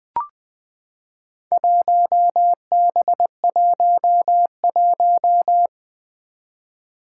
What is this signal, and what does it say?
Morse code, heard as a keyed tone, about 21 wpm, 700 Hz. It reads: 1B11
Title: Four Challenging Letter-Number Combinations 20wpm